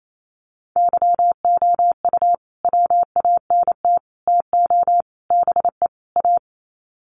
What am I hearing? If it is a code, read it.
Morse code, heard as a keyed tone, about 28 wpm, 700 Hz. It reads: YOU WANT TO BE A
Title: Sets of 5 Words Encore2 28wpm